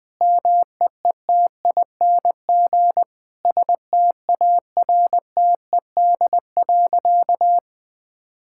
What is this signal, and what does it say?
Morse code, heard as a keyed tone, about 20 wpm, 700 Hz. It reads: MEETING STARTED.
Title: Sentences with 5-Word Phrases - 20wpm